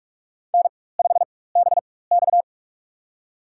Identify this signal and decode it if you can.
Morse code, heard as a keyed tone, about 46 wpm, spaced as for 21 wpm, 700 Hz. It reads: N5BX
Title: USA Call Signs - 4 Characters - 4x Character Spacing - 45wpm